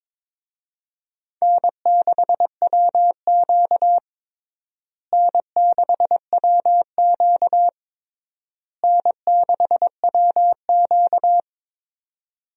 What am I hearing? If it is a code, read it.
Morse code, heard as a keyed tone, about 21 wpm, 700 Hz. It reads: N6WQ N6WQ N6WQ
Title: USA Call Signs - 4 Characters - Repeated 3x - 22wpm